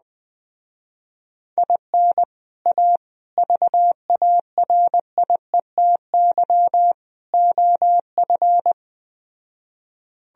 Morse code, heard as a keyed tone, about 20 wpm, 700 Hz. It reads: IN A VARIETY OF